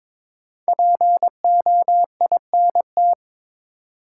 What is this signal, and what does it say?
Morse code, heard as a keyed tone, about 22 wpm, 700 Hz. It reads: POINT
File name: 5 Letter Words - First 3 Letters Given - 22wpm